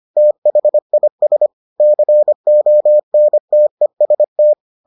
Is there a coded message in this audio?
Yes: Morse code from 0.2 to 4.5 s, heard as a keyed tone, about 25 wpm, 600 Hz. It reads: THIS CONTEST